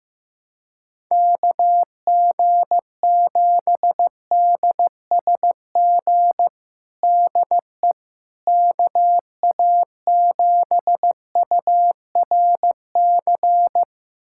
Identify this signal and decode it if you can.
Morse code, heard as a keyed tone, about 16 wpm, 700 Hz. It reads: KG7DSG DE KA7URC
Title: USA Callsign DE Callsign - 6 Characters - 15wpm